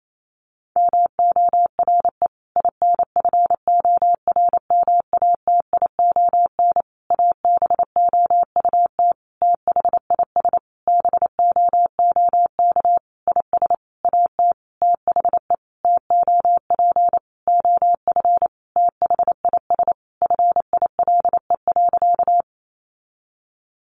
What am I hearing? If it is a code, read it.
Morse code, heard as a keyed tone, about 28 wpm, 700 Hz. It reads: MORE INFORMATION ABOUT THIS BOOK IS AT THE TOP OF THIS FILE.
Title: Sentences from Top 1200 Words 28wpm